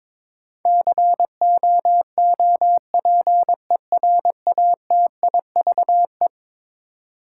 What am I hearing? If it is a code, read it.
Morse code, heard as a keyed tone, about 22 wpm, 700 Hz. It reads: COOPERATIVE
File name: practice / 22wpm / Eleven Letter Words - 22wpm